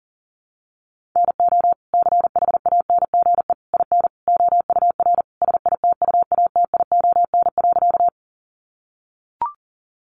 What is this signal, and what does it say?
Morse code, heard as a keyed tone, about 40 wpm, 700 Hz. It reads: NO CHANGE IN OUR SITUATION.
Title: Sentences from Top 1200 Words 40wpm